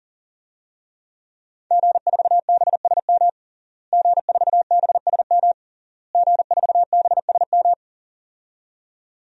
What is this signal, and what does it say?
Morse code, heard as a keyed tone, about 37 wpm, 700 Hz. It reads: G4BSM G4BSM G4BSM
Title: DXCC-100 Call Signs - 5 Characters - Repeated 3x - 40wpm